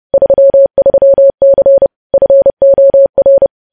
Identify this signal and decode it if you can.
Morse code, heard as a keyed tone, about 30 wpm, 550 Hz. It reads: 33C FOR
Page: news headlines in morse code at 30 wpm